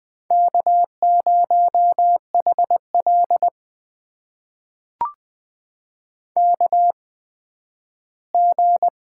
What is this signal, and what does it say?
Morse code, heard as a keyed tone, about 20 wpm, spaced as for 12 wpm, 700 Hz. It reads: K0HL KG
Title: USA Call Signs - 8x Character Spacing - 20wpm